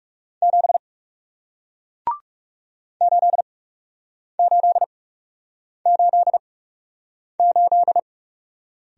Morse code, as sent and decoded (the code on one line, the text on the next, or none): --... / ---.. ---.. ---.. ---..
7 8888